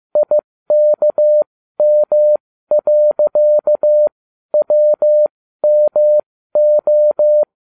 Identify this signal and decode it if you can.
Morse code, heard as a keyed tone, about 15 wpm, spaced as for 12 wpm, 600 Hz. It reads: IKM.WMO